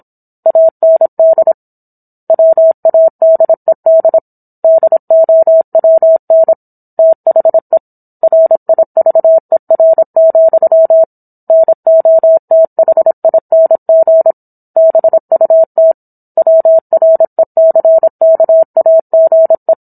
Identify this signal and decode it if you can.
Morse code, heard as a keyed tone, about 27 wpm, 650 Hz. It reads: AND WADED DOWN THE RIVER, NOTHING BUT WRECKAGE